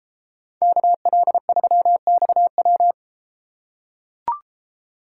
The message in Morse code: KL3XW

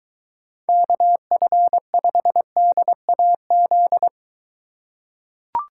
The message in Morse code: KF5DAZ

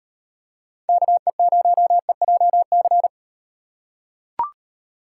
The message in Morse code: KE0EJC

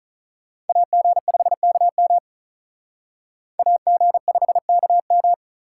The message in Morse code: AG5KM AG5KM

